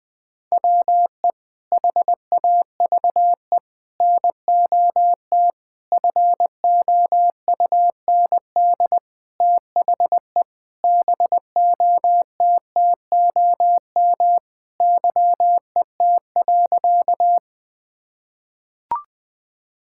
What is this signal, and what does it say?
Morse code, heard as a keyed tone, about 20 wpm, 700 Hz. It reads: WE HAVE NOT FOUND THE BOTTOM YET.
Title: Sentences from Top 1000 Words - Review Part 6 - 20wpm